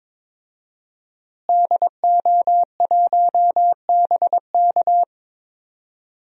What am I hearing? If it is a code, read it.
Morse code, heard as a keyed tone, about 22 wpm, 700 Hz. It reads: DO1BK
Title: DXCC-100 Call Signs - 5 Characters - Repeated 3x - 22wpm